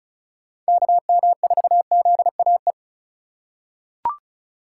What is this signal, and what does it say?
Morse code, heard as a keyed tone, about 34 wpm, 700 Hz. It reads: KM4ZAE
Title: USA Call Signs - 6 Characters - 3x Character Spacing - 35wpm